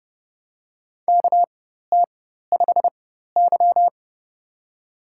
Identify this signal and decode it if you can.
Morse code, heard as a keyed tone, about 30 wpm, spaced as for 14 wpm, 700 Hz. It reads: KT5Y
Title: USA Call Signs - 4 Characters - 4x Character Spacing - 30wpm